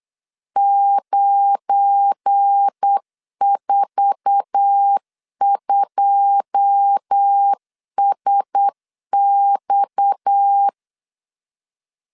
Morse code, heard as a keyed tone, about 8 wpm, 800 Hz. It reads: 942SX
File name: LEZIONI / 9 2 8wpm 3437